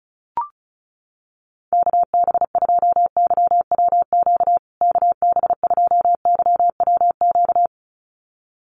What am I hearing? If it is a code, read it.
Morse code, heard as a keyed tone, about 34 wpm, 700 Hz. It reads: KB2YWQ KB2YWQ
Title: USA Call Signs Repeated 2x 35wpm